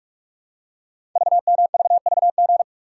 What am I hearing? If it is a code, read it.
Morse code, heard as a keyed tone, about 44 wpm, 690 Hz. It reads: VMVVG
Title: Code Groups - 5 Letters - 45wpm